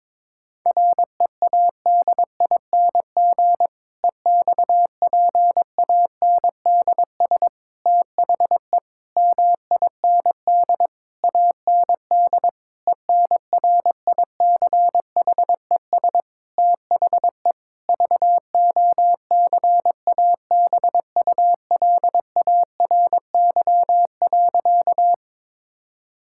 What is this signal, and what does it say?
Morse code, heard as a keyed tone, about 22 wpm, 700 Hz. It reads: READING EXPANDS THE MIND AND ENRICHES THE VOCABULARY.
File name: Difficult Sentences with Hints - 22wpm